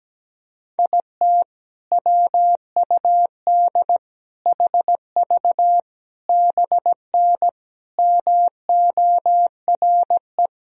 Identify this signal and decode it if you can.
Morse code, heard as a keyed tone, about 17 wpm, 700 Hz. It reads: IT WUD HV BN MORE
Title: Sentences from Top 400 Words and Abbreviations 17wpm